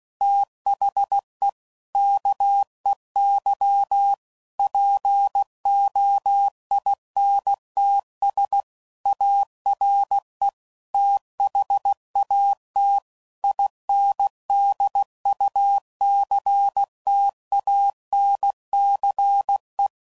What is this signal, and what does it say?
Morse code, heard as a keyed tone, about 16 wpm, 800 Hz. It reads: THE KEY POINTS ARE THAT INDUCTANCE
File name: CW Code Groups @15WPM / Technical Subjects / Reactance